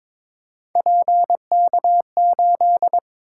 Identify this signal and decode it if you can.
Morse code, heard as a keyed tone, about 22 wpm, 700 Hz. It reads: PK8
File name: practice / 22wpm / Three Letter-Number Combinations - 22wpm